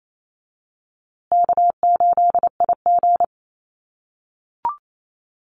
Morse code, heard as a keyed tone, about 28 wpm, 700 Hz. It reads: K8IG
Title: USA Call Signs Repeated 3x 28wpm